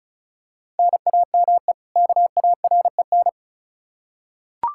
Morse code, heard as a keyed tone, about 34 wpm, 700 Hz. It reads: NAME KAREN
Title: Practice QSO - Exchange 1 - Beginning - 35wpm